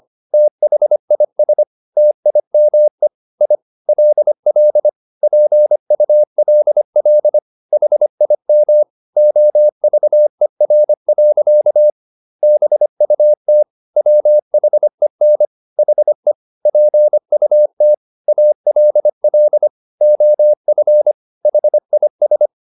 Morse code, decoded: THIS TIME I LL PULL HIM OVER. BUT WHEN HE PUT ALL OF HIS

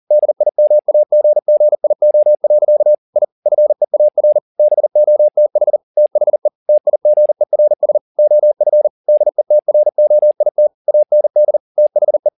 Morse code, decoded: DIMAGGIO. I FEAR BOTH THE TIGERS OF DETROIT AND THE

